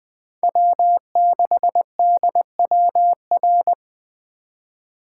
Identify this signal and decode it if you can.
Morse code, heard as a keyed tone, about 20 wpm, 700 Hz. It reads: W6DWR
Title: USA Call Signs - 5 Characters - 3x Character Spacing - 20wpm